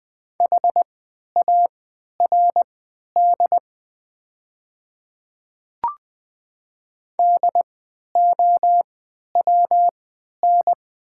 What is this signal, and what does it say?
Morse code, heard as a keyed tone, about 20 wpm, spaced as for 11 wpm, 700 Hz. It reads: HARD DOWN
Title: 3x Character Spacing - Top 300 Words - 20wpm